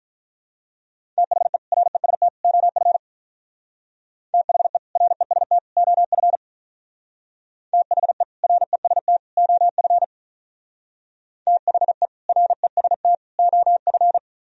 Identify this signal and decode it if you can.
Morse code, heard as a keyed tone, about 45 wpm, 700 Hz. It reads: THE REST OF THE REST OF THE REST OF THE REST OF